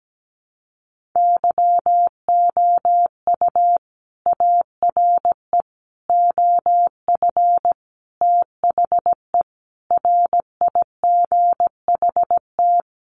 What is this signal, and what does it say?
Morse code, heard as a keyed tone, about 17 wpm, 700 Hz. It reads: YOU ARE OF THE RIGHT
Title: Sentences from Top 1400 Words 17wpm